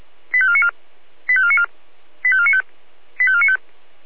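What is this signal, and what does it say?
Five-tone selective call (ZVEI), heard as quick runs of five tones, tones of about 70 ms. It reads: 85494 85494 85495 85495
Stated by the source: ZVEI five-tone selective call 85492 received with an SDR.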